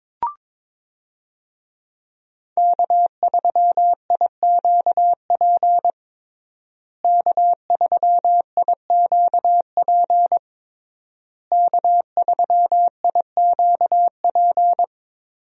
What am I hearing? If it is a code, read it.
Morse code, heard as a keyed tone, about 22 wpm, 700 Hz. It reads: K3IQP K3IQP K3IQP
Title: USA Call Signs - 5 Characters - Repeated 3x - 22wpm